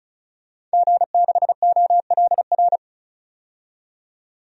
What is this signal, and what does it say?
Morse code, heard as a keyed tone, about 36 wpm, 700 Hz. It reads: G6OLR
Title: DXCC-100 Call Signs - 5 Characters - 35wpm